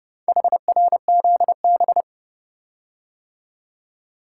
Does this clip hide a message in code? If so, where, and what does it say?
Yes: Morse code from 0.3 to 2.0 s, heard as a keyed tone, about 29 wpm, 700 Hz. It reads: HRZB